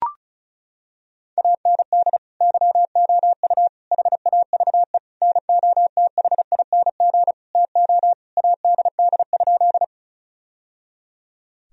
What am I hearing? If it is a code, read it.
Morse code, heard as a keyed tone, about 36 wpm, 700 Hz. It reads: AND YOU HAVE NOTHING TO ADD?
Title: Sentences from Top 1000 Words - Review Part 1 - 35wpm